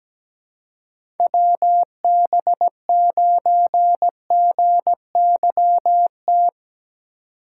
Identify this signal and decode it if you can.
Morse code, heard as a keyed tone, about 17 wpm, 700 Hz. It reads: WB9GYT
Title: USA Call Signs - 6 Characters - 3x Character Spacing - 17wpm